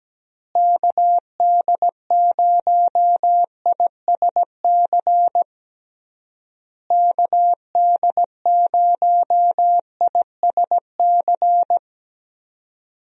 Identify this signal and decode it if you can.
Morse code, heard as a keyed tone, about 17 wpm, 700 Hz. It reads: KD0ISC KD0ISC
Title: USA Call Signs - 6 Characters - Repeated 3x - 17wpm